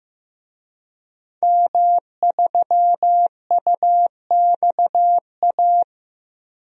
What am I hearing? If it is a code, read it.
Morse code, heard as a keyed tone, about 15 wpm, 700 Hz. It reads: M3UXA
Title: DXCC-100 Call Signs - 5 Characters - Repeated 2x - 15wpm